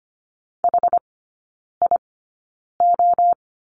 Morse code, heard as a keyed tone, about 25 wpm, spaced as for 9 wpm, 700 Hz. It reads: HIO
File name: Lesson 14e - Number 2 - Three Characters 25wpm